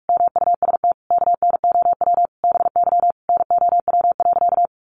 Morse code, heard as a keyed tone, about 44 wpm, 700 Hz. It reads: MUST KNOW BY NOW.
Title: Sentences from Top 1400 Words 45wpm